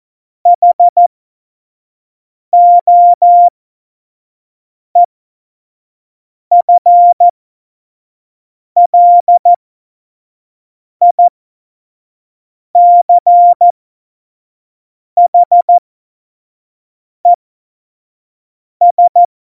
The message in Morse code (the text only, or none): HOEFLICHES